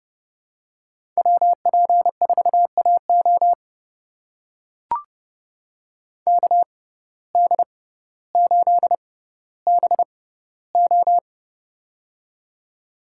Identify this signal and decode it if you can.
Morse code, heard as a keyed tone, about 31 wpm, spaced as for 16 wpm, 700 Hz. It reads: WP4AO KD8BO
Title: USA Call Signs - 5 Characters - 6x Character Spacing - 30wpm